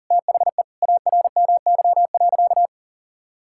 Morse code, heard as a keyed tone, about 40 wpm, 700 Hz. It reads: THE ARMY.
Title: Sentences from Top 1000 Words - Review Part 6 - 40wpm